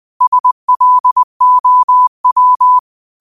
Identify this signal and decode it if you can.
Morse code, heard as a keyed tone, about 20 wpm, 1000 Hz. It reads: SLOW
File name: qotdfiles / 20wpm